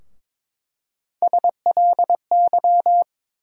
Morse code, heard as a keyed tone, about 22 wpm, 700 Hz. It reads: SLY
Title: USA Call Sign Postfixes - 3 Letters 22wpm